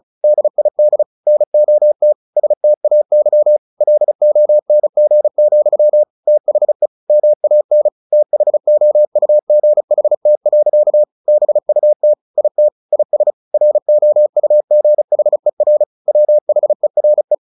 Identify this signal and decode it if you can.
Morse code, heard as a keyed tone, about 34 wpm, 600 Hz. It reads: DID NOT STAY LONG, THE MAN THOUGHT. BUT IT IS ROUGHER WHERE
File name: the old man and the sea / the old man and the sea 1 35wpm